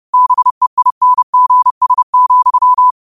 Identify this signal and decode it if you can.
Morse code, heard as a keyed tone, about 30 wpm, 1000 Hz. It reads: BEINGS,
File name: qotdfiles / 30wpm